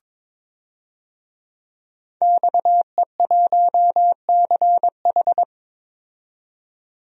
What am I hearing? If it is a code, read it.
Morse code, heard as a keyed tone, about 22 wpm, 700 Hz. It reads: XE1CH
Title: DXCC-100 Call Signs - 5 Characters - 22wpm